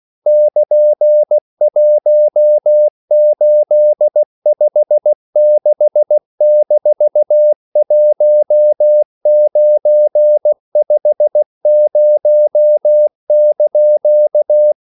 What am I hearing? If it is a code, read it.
Morse code, heard as a keyed tone, about 16 wpm, 600 Hz. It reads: <KN>1856-1950)
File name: CW / Practice / EnglishQuotations0116